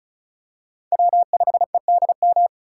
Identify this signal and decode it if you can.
Morse code, heard as a keyed tone, about 36 wpm, 700 Hz. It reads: W5EDM
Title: USA Call Signs - 5 Characters - 3x Character Spacing - 35wpm